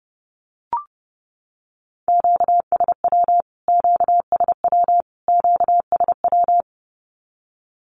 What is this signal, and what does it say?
Morse code, heard as a keyed tone, about 28 wpm, 700 Hz. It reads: QSW QSW QSW
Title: Q-Codes Repeated 3x 30wpm